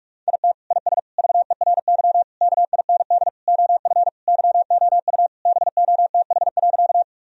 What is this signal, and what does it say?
Morse code, heard as a keyed tone, about 42 wpm, 700 Hz. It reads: IT IS VERY KIND OF YOU BOTH.
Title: Sentences from Top 1000 Words - Review Part 1 - 45wpm